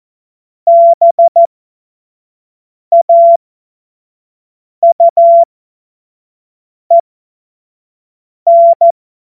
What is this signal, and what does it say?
Morse code, heard as a keyed tone, about 14 wpm, spaced as for 5 wpm, 680 Hz. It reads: BAUEN
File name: Morsekurs / Witze / Witz 08. August 2023 25 F70